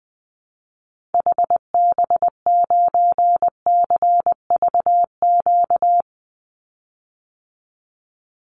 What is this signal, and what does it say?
Morse code, heard as a keyed tone, about 20 wpm, 700 Hz. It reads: HB9CVQ
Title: Call Signs Repeated 3x - 20wpm